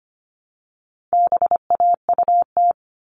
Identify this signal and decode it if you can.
Morse code, heard as a keyed tone, about 25 wpm, 700 Hz. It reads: BAUT
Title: All Cut Numbers - 4-Digit 25wpm